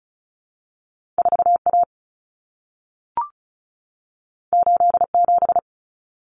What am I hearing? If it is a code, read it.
Morse code, heard as a keyed tone, about 36 wpm, 700 Hz. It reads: 4A 87
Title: Common Cut Numbers - 2-Digit 35wpm